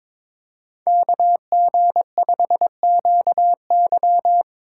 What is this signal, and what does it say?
Morse code, heard as a keyed tone, about 22 wpm, 700 Hz. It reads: KG5QY
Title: USA Call Signs - 5 Characters - 3x Character Spacing - 22wpm